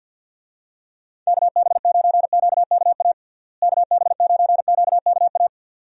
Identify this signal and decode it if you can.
Morse code, heard as a keyed tone, about 53 wpm, 700 Hz. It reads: KB9QKA KB9QKA
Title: USA Call Signs - 6 Characters - Repeated 2x - 50wpm